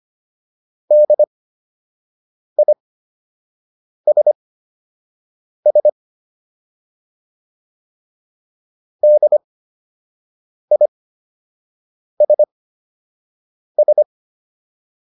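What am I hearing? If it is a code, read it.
Morse code, heard as a keyed tone, about 25 wpm, spaced as for 6 wpm, 600 Hz. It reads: DISS DISS
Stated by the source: Fundamental class Morse code practice material.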